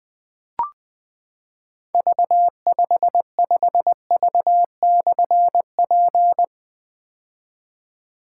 Morse code, heard as a keyed tone, about 20 wpm, 700 Hz. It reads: V55V/P
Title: Difficult Call Signs 20wpm